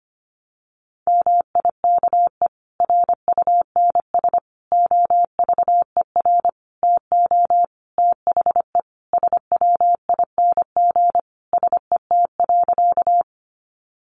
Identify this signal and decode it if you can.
Morse code, heard as a keyed tone, about 25 wpm, 700 Hz. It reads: MIKE RUNS OVER TO THE SWING SET.
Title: Additional Easy Phrases 25wpm